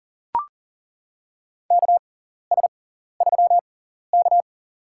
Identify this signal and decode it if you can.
Morse code, heard as a keyed tone, about 40 wpm, spaced as for 14 wpm, 700 Hz. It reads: KS3K
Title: USA Call Signs - 4 Characters - 6x Character Spacing - 40wpm